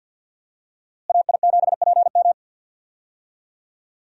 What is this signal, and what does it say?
Morse code, heard as a keyed tone, about 46 wpm, 700 Hz. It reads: AI7PM
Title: USA Call Signs - 5 Characters - Repeated 3x - 50wpm